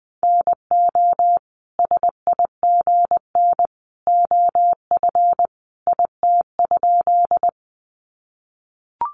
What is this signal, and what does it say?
Morse code, heard as a keyed tone, about 20 wpm, 700 Hz. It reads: NO SIGN OF IT?
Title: Sentences from Top 1000 Words 20wpm